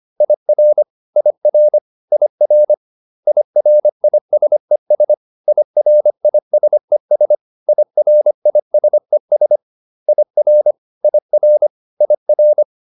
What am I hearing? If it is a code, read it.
Morse code, heard as a keyed tone, about 25 wpm, 600 Hz. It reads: IR IR IR IRISES IRISES IRISES IR IR IR